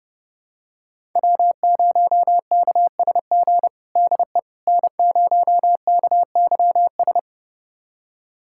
Morse code, heard as a keyed tone, about 30 wpm, 700 Hz. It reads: W0KSG DE N0KYS
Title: USA Callsign DE Callsign - 5 Characters - 30wpm